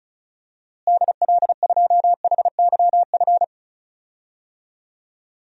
Morse code, heard as a keyed tone, about 36 wpm, 700 Hz. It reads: DL2HYF
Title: DXCC-100 Call Signs - 6 Characters - Repeated 2x - 35wpm